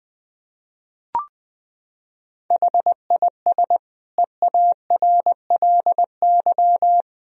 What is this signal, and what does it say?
Morse code, heard as a keyed tone, about 20 wpm, 700 Hz. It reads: HIS EARLY
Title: Sentences from Top 1000 Words - Review Part 4 - 20wpm